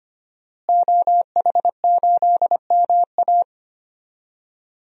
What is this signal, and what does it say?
Morse code, heard as a keyed tone, about 25 wpm, 700 Hz. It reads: OH8MA
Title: DXCC-100 Call Signs - 5 Characters - Repeated 3x - 25wpm